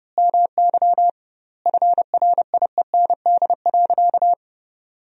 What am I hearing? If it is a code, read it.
Morse code, heard as a keyed tone, about 30 wpm, 700 Hz. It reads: MY FRIEND.
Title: 2x Word-Spacing - Sentences from Top 300 Words - 30wpm